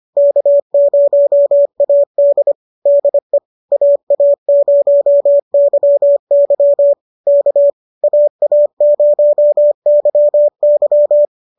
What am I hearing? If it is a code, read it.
Morse code, heard as a keyed tone, about 25 wpm, 570 Hz. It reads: K0AD DE AA0YY K AA0YY